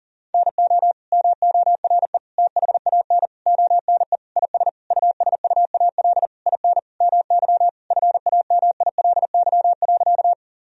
Morse code, heard as a keyed tone, about 40 wpm, 700 Hz. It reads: NO MORE THAN ONE IS USUAL IN MY FAMILY.
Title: Sentences from Top 1000 Words - Review Part 2 - 40wpm